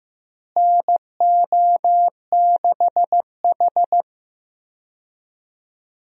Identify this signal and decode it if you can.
Morse code, heard as a keyed tone, about 15 wpm, 700 Hz. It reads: NO6H
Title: USA Call Signs - 4 Characters - Repeated 2x - 15wpm